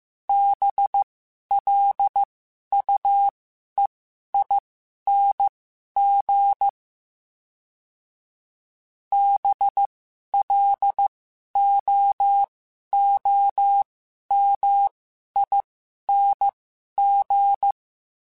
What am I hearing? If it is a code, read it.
Morse code, heard as a keyed tone, about 15 wpm, spaced as for 11 wpm, 800 Hz. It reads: BLUEING BLOOMING
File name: Special INGs-mp3s / Special INGs-mp3s / Sngl Ing @10